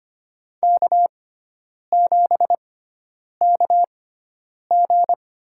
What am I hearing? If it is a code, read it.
Morse code, heard as a keyed tone, about 25 wpm, spaced as for 9 wpm, 700 Hz. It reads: K7KG
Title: USA Call Signs - 4 Characters - 6x Character Spacing - 25wpm